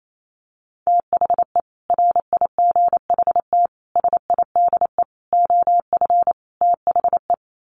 Morse code, heard as a keyed tone, about 28 wpm, 700 Hz. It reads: THE RIGHT SIDE OF THE
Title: Sets of 5 Words Encore 28wpm